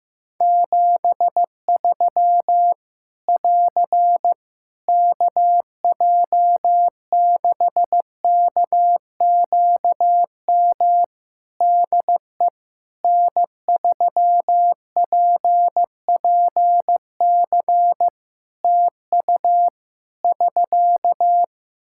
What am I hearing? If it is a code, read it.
Morse code, heard as a keyed tone, about 15 wpm, 700 Hz. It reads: 73 <AR> KJ6KQM DE N3PPC TU <SK>
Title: K1USN SST QSO Practice 15wpm